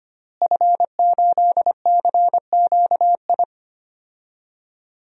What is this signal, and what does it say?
Morse code, heard as a keyed tone, about 25 wpm, 700 Hz. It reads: F8CQI
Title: DXCC-100 Call Signs - 5 Characters - Repeated 3x - 25wpm